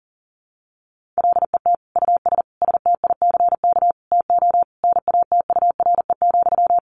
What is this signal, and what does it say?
Morse code, heard as a keyed tone, about 40 wpm, 700 Hz. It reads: LET US STICK TO NATURE,
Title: Sentences from Top 1200 Words 40wpm